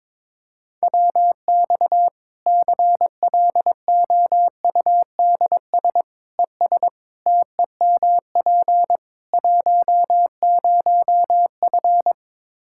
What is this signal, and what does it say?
Morse code, heard as a keyed tone, about 22 wpm, 700 Hz. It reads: WX CLOUDS ES TEMP 10F
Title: Practice QSO - Exchange 2 - Info - 22wpm